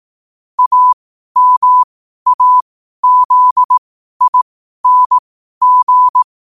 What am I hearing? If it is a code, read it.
Morse code, heard as a keyed tone, about 18 wpm, spaced as for 12 wpm, 1000 Hz. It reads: AMAZING
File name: qotdfiles / 12wpm